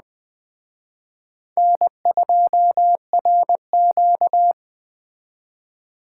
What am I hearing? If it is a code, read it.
Morse code, heard as a keyed tone, about 20 wpm, 700 Hz. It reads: N2RQ
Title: Speed Racing - USA 1x2 Call Signs 17wpm